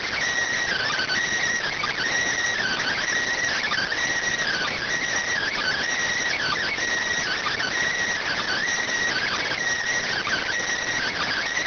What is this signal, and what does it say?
Emergency Alert System (SAME) tones (header bursts), heard as harsh buzzing bursts. It is heard from 0.6 to 10.6 s.